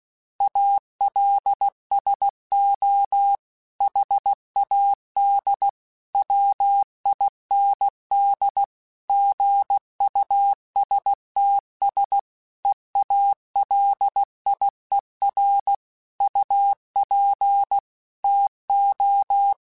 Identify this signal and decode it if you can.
Morse code, heard as a keyed tone, about 16 wpm, 800 Hz. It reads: ALSO HAD WIND GUSTS EALIER UP TO
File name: Weather Sets / Weather Sets-mp3s / Weather 5 5 @15